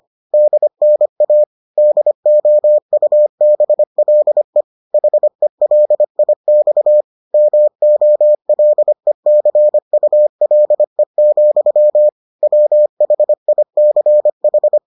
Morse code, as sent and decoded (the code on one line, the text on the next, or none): -.. -. .- / -.. --- ..- -... .-.. / .... .-.. .. -..- / -- --- .-.. -.-. ..- .-.. --..-- / .-- .... .. -.-. ....
DNA DOUBLE HELIX MOLECULE, WHICH